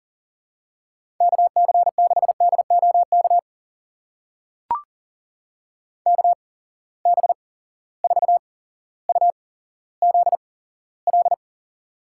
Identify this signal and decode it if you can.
Morse code, heard as a keyed tone, about 39 wpm, spaced as for 17 wpm, 700 Hz. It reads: KC6DOK KB4UZL
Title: USA Call Signs - 6 Characters - 8x Character Spacing - 40wpm